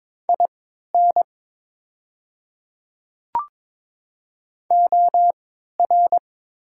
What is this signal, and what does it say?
Morse code, heard as a keyed tone, about 23 wpm, spaced as for 13 wpm, 700 Hz. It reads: IN OR
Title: 3x Character Spacing - Top 100 Words - 22wpm